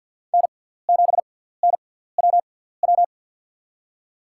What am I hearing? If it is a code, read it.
Morse code, heard as a keyed tone, about 52 wpm, spaced as for 17 wpm, 700 Hz. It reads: N7NWW
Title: USA Call Signs - 5 Characters - 6x Character Spacing - 50wpm